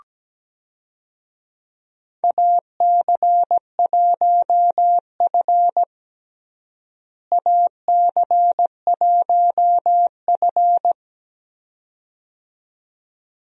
Morse code, heard as a keyed tone, about 17 wpm, 700 Hz. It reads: AC1F AC1F
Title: USA Call Signs - 4 Characters - Repeated 2x - 17wpm